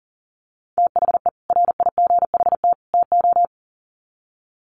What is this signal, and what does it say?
Morse code, heard as a keyed tone, about 40 wpm, 700 Hz. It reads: THE RIGHT TO
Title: Sets of 3 Words 40wpm